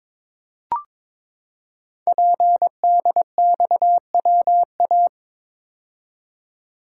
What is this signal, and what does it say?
Morse code, heard as a keyed tone, about 22 wpm, 700 Hz. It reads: PDXWA
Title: Code Groups - 5 Letter:Numbers - 22wpm